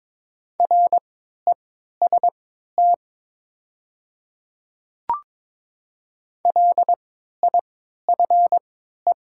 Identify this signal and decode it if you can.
Morse code, heard as a keyed tone, about 21 wpm, spaced as for 12 wpm, 700 Hz. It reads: REST LIFE
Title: 3x Character Spacing - Top 500 Words - 22wpm